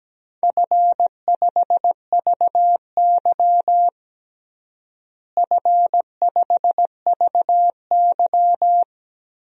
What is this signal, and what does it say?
Morse code, heard as a keyed tone, about 17 wpm, 700 Hz. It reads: F5VY F5VY
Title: DXCC-100 Call Signs - 4 Characters - Repeated 3x - 17wpm